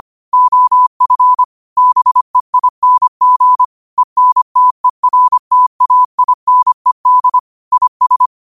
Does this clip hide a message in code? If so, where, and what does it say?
Yes: Morse code from 0.3 to 8.3 s, heard as a keyed tone, about 25 wpm, 1000 Hz. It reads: OF BEING ENTERTAINED IS